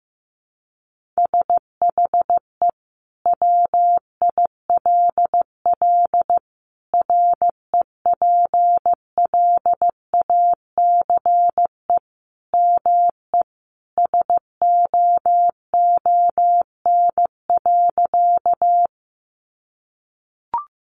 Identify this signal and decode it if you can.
Morse code, heard as a keyed tone, about 15 wpm, 700 Hz. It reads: SHE WILL REPLACE ME SOON.
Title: Sentences from Top 1400 Words 15wpm